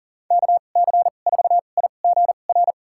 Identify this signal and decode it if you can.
Morse code, heard as a keyed tone, about 40 wpm, spaced as for 29 wpm, 700 Hz. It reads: KC4IGR
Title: USA Call Signs - 6 Characters - 2x Character Spacing - 40wpm